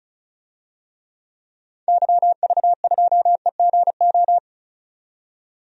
Morse code, heard as a keyed tone, about 36 wpm, 700 Hz. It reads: YV2EGO
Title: DXCC-100 Call Signs - 6 Characters - 35wpm